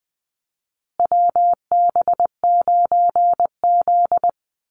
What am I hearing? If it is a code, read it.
Morse code, heard as a keyed tone, about 20 wpm, 700 Hz. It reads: WB9Z
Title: Call Signs Repeated 2x - 20wpm